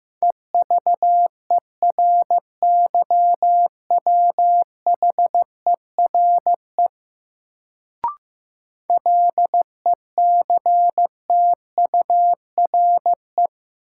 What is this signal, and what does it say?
Morse code, heard as a keyed tone, about 15 wpm, 700 Hz. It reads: EVERYWHERE LECTURE
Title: Sentences from Top 1600 Words - New Word Drill - 15wpm